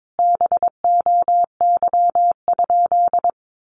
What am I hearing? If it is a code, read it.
Morse code, heard as a keyed tone, about 22 wpm, 700 Hz. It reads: BOY?
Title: Sentences from Top 1500 Words 22wpm